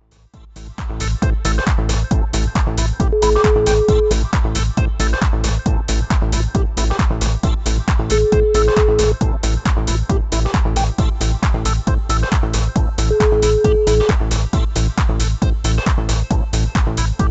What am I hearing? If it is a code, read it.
Ringback tone (Europe, 425 Hz), with noise.